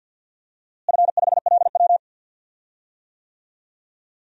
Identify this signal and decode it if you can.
Morse code, heard as a keyed tone, about 50 wpm, 700 Hz. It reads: F5LW
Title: DXCC-100 Call Signs - 4 Characters - Repeated 3x - 50wpm